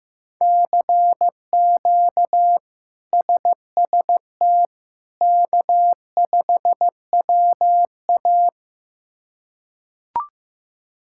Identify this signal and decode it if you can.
Morse code, heard as a keyed tone, about 15 wpm, 700 Hz. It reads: CQ SST K5WA
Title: K1USN SST QSO Practice - 15wpm